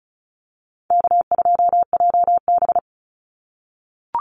Morse code, heard as a keyed tone, about 34 wpm, 700 Hz. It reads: K2JB
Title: NA SOTA Activator Callsigns 35wpm